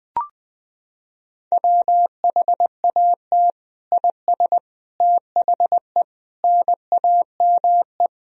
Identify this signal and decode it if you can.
Morse code, heard as a keyed tone, about 20 wpm, 700 Hz. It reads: WHAT IS THE NAME
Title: Sentences from Top 1000 Words - Review Part 1 - 20wpm